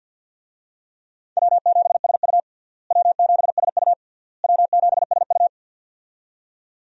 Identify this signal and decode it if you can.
Morse code, heard as a keyed tone, about 51 wpm, 700 Hz. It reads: W7SU W7SU W7SU
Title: USA Call Signs - 4 Characters - Repeated 3x - 50wpm